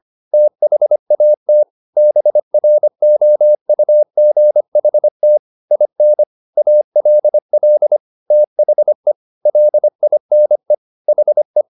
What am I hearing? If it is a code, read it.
Morse code, heard as a keyed tone, about 25 wpm, 600 Hz. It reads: THAT BROUGHT IN ALL THE LINE HE